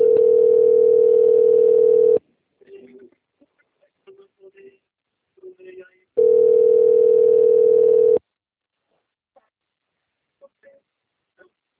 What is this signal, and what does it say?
Ringback tone (North America), 440 + 480 Hz, with speech.
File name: out- - - - - .4975